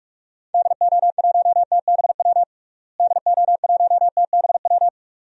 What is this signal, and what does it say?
Morse code, heard as a keyed tone, about 41 wpm, 690 Hz. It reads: DO1TBW DO1TBW
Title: DXCC-100 Call Signs - 6 Characters - Repeated 2x - 45wpm